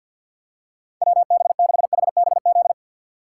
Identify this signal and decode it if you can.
Morse code, heard as a keyed tone, about 54 wpm, 700 Hz. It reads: WB6HBZ
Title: USA Call Signs - 6 Characters - 6x Character Spacing - 50wpm